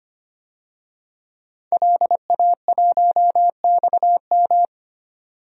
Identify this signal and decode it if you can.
Morse code, heard as a keyed tone, about 25 wpm, 700 Hz. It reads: LA1XM
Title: DXCC-100 Call Signs - 5 Characters - 25wpm